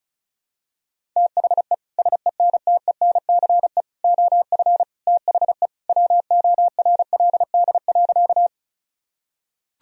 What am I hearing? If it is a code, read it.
Morse code, heard as a keyed tone, about 34 wpm, 700 Hz. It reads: THE SENTENCE OF THE WORLD.
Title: Sentences from Top 1000 Words - Review Part 1 - 35wpm